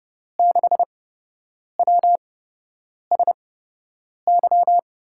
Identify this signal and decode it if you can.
Morse code, heard as a keyed tone, about 31 wpm, spaced as for 8 wpm, 700 Hz. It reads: 6WSY
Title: USA Call Signs - 6 Characters - 8x Character Spacing - 30wpm